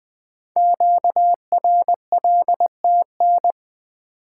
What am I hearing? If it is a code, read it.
Morse code, heard as a keyed tone, about 20 wpm, 700 Hz. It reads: QRLTN
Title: Code Groups - 5 Letters - 20wpm